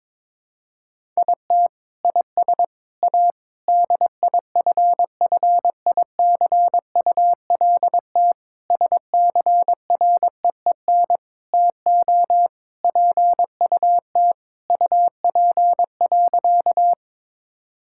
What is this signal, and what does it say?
Morse code, heard as a keyed tone, about 22 wpm, 700 Hz. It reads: IT IS A DIFFICULT SCREEN TO PUT UP.
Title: Sentences from Top 1000 Words - Review Part 6 - 22wpm